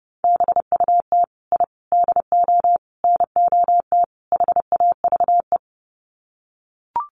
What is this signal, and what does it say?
Morse code, heard as a keyed tone, about 28 wpm, 700 Hz. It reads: BUT I DO NOT HAVE
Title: Sets of 5 Words Encore 30wpm